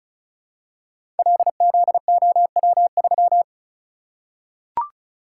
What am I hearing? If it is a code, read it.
Morse code, heard as a keyed tone, about 36 wpm, 700 Hz. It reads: LZOW3
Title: Code Groups - 5 Letter:Numbers - 35wpm